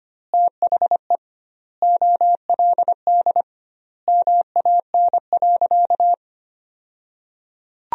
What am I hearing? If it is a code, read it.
Morse code, heard as a keyed tone, about 25 wpm, 700 Hz. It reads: THE OLD MAN.
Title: 2x Word-Spacing - Sentences from Top 200 Words - 25wpm